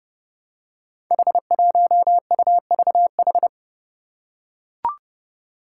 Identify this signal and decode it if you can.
Morse code, heard as a keyed tone, about 30 wpm, 700 Hz. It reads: H1UVH
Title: Five Challenging Letter-Number Combinations 30wpm